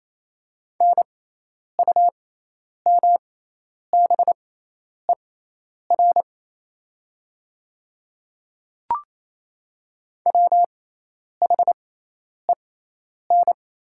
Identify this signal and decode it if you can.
Morse code, heard as a keyed tone, about 28 wpm, spaced as for 10 wpm, 700 Hz. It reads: NUMBER WHEN
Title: 6x Character Spacing - Top 100 Words - 28wpm